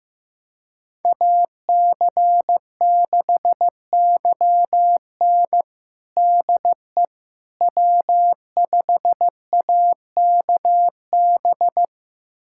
Morse code, heard as a keyed tone, about 15 wpm, 700 Hz. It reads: AC6YN DE W5AKB
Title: USA Callsign DE Callsign - 5 Characters - 15wpm